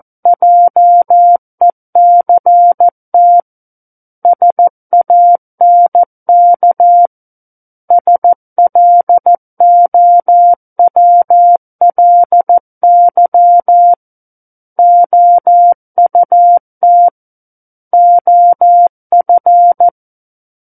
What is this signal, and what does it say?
Morse code, heard as a keyed tone, about 14 wpm, 700 Hz. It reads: JECT SANK SLOWLY OUT OF
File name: Book One / 05-Chapter 5 - 14 WPM